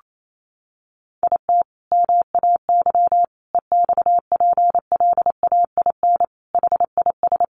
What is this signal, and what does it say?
Morse code, heard as a keyed tone, about 28 wpm, 700 Hz. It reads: IT MAY EXPLAIN HIS
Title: Sentences from Top 1300 Words 28wpm